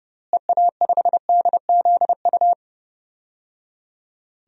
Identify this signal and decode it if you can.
Morse code, heard as a keyed tone, about 30 wpm, 700 Hz. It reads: EA5DZU
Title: DXCC-100 Call Signs - 6 Characters - Repeated 3x - 30wpm